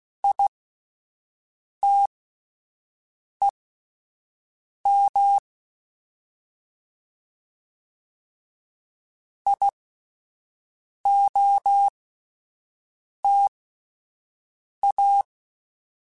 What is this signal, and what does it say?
Morse code, heard as a keyed tone, about 16 wpm, spaced as for 5 wpm, 800 Hz. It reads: ITEM IOTA